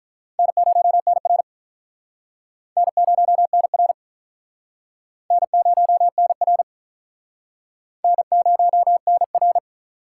Morse code, heard as a keyed tone, about 46 wpm, 700 Hz. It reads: N0NR N0NR N0NR N0NR